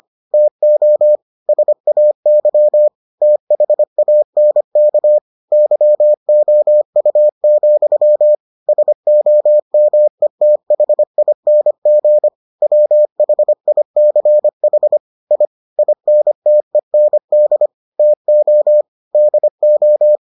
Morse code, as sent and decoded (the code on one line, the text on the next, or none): - --- / ... .- -.-- / - .... .- -. -.- / -.-- --- ..- --..-- / ... --- -- . - .... .. -. --. / .-- .... .. -.-. .... / .. / .. -. - . -. -.. / - --- / -.. ---
TO SAY THANK YOU, SOMETHING WHICH I INTEND TO DO